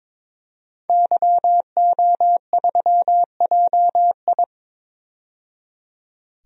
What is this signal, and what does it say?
Morse code, heard as a keyed tone, about 22 wpm, 700 Hz. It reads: YO3JI
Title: DXCC-100 Call Signs - 5 Characters - 22wpm